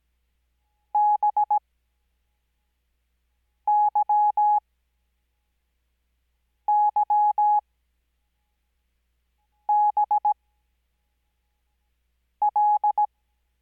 Morse code, heard as a keyed tone, about 17 wpm, spaced as for 4 wpm, 840 Hz. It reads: BYYBL
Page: Farnsworth Spacing: 15WPM Character speed Spaced at 5WPM.